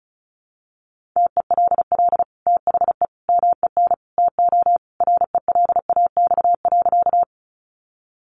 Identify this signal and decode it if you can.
Morse code, heard as a keyed tone, about 34 wpm, 700 Hz. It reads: TELL THE MEN TO RELAX.
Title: Sentences from Top 1400 Words 35wpm